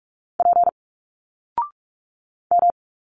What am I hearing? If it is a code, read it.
Morse code, heard as a keyed tone, about 45 wpm, 700 Hz. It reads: P M
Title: Single Letters 45wpm